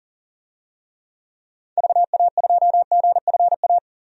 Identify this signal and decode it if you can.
Morse code, heard as a keyed tone, about 40 wpm, 700 Hz. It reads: VA2GFA
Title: DXCC-100 Call Signs - 6 Characters - 40wpm